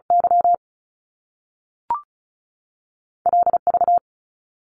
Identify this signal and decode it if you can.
Morse code, heard as a keyed tone, about 36 wpm, 700 Hz. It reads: Y L V
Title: Top Two Letter Combinations in Words 35wpm